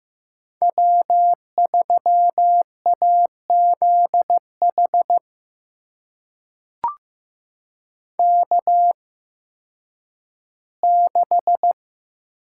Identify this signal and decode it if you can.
Morse code, heard as a keyed tone, about 15 wpm, spaced as for 10 wpm, 700 Hz. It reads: W3AZH K6
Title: USA Call Signs - 5 Characters - 8x Character Spacing - 15wpm